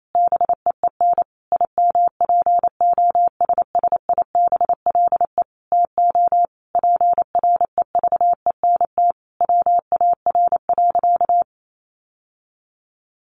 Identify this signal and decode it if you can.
Morse code, heard as a keyed tone, about 28 wpm, 700 Hz. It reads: BEEN IMPOSSIBLE TO PREVENT WAR.
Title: Sentences from Top 1100 Words 28wpm